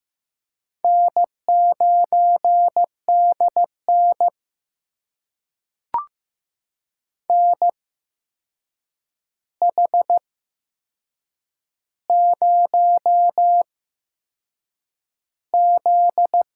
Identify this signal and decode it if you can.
Morse code, heard as a keyed tone, about 15 wpm, spaced as for 6 wpm, 700 Hz. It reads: N9DN NH0Z
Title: USA Call Signs - 4 Characters - 8x Character Spacing - 15wpm